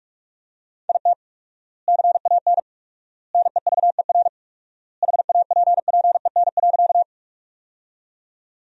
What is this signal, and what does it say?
Morse code, heard as a keyed tone, about 46 wpm, 700 Hz. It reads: IT CAN NEVER HAPPEN.
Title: 4x Word-Spacing - Sentences from Top 400 Words - 45wpm